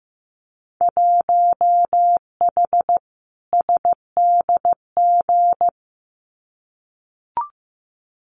Morse code, heard as a keyed tone, about 15 wpm, 700 Hz. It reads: 1H SDG
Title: Winter Field Day Exchanges 15wpm